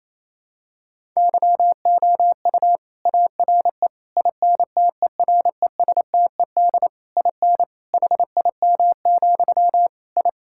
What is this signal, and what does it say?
Morse code, heard as a keyed tone, about 28 wpm, 700 Hz. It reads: YOU ARE INTERESTED IN HIM, I
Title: Sentences from Top 1000 Words - Review Part 6 - 28wpm